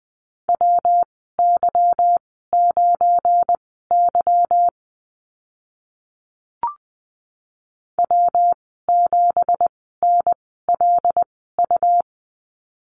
Morse code, heard as a keyed tone, about 20 wpm, spaced as for 14 wpm, 700 Hz. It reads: WY9Y W7NLU
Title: 193 Lesson 31g - Letter v - Call Signs 20wpm